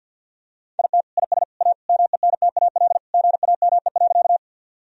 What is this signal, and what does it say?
Morse code, heard as a keyed tone, about 50 wpm, 700 Hz. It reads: IT IS A MENTAL GAME.
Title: Sentences from Top 1000 Words - Review Part 4 - 50wpm